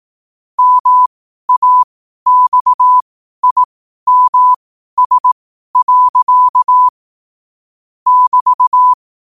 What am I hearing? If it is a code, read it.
Morse code, heard as a keyed tone, about 18 wpm, spaced as for 12 wpm, 1000 Hz. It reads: MAXIMS. <BT>
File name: qotdfiles / 12wpm